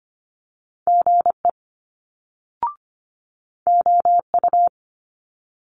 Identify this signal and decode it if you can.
Morse code, heard as a keyed tone, about 25 wpm, 700 Hz. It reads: GE OU
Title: Multi-letter Phonemes Quick Fire 25wpm